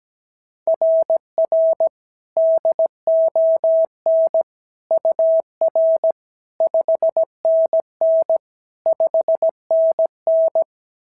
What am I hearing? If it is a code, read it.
Morse code, heard as a keyed tone, about 16 wpm, 650 Hz. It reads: RR DON UR 5NN 5NN
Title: SOTA QSO Practice 17wpm